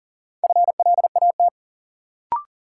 Morse code, heard as a keyed tone, about 38 wpm, 700 Hz. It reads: FLAT
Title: Sentences from Top 1000 Words - New Word Drill - 40wpm